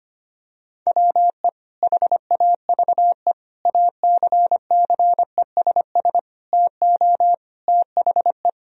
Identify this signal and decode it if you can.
Morse code, heard as a keyed tone, about 25 wpm, 700 Hz. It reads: WE HAVE ACCESS TO THE
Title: Sentences from Top 1000 Words - Review Part 4 - 25wpm